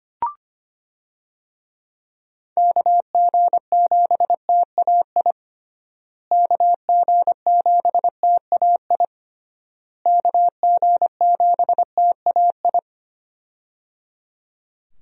Morse code, heard as a keyed tone, about 25 wpm, 700 Hz. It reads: KG7TAI KG7TAI KG7TAI
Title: USA Call Signs - 6 Characters - Repeated 3x - 25wpm